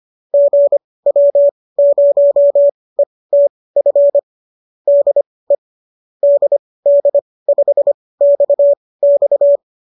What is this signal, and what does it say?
Morse code, heard as a keyed tone, about 25 wpm, spaced as for 18 wpm, 590 Hz. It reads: GW0ETF DE DD5XX
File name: qso218 18